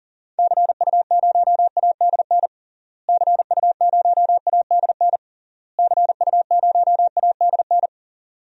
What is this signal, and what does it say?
Morse code, heard as a keyed tone, about 37 wpm, 700 Hz. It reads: CU0ADN CU0ADN CU0ADN
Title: DXCC-100 Call Signs - 6 Characters - Repeated 3x - 40wpm